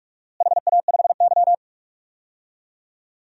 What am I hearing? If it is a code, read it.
Morse code, heard as a keyed tone, about 46 wpm, 700 Hz. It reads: HA5Y
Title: DXCC-100 Call Signs - 4 Characters - Repeated 3x - 45wpm